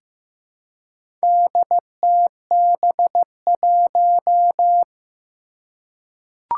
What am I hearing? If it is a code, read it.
Morse code, heard as a keyed tone, about 15 wpm, 700 Hz. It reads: DTB1
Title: Four Challenging Letter-Number Combinations 15wpm